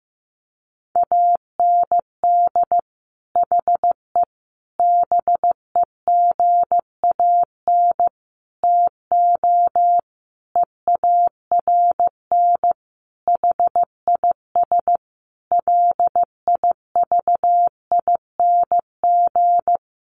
Morse code, heard as a keyed tone, about 15 wpm, 700 Hz. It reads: AND HE BEGAN TO EARN HIS LIVING
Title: Sentences from Top 1500 Words 15wpm